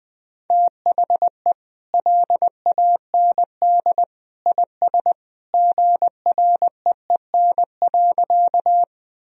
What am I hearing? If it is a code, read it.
Morse code, heard as a keyed tone, about 21 wpm, 700 Hz. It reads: THE LAND IS GREEN.
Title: Sentences from Top 1000 Words - Review Part 2 - 20wpm